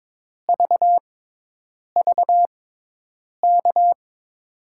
Morse code, heard as a keyed tone, about 22 wpm, spaced as for 8 wpm, 700 Hz. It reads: VVK
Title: USA Call Signs - 5 Characters - 6x Character Spacing - 22wpm